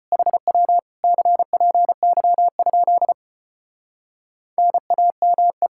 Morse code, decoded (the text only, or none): HW CPY? NAME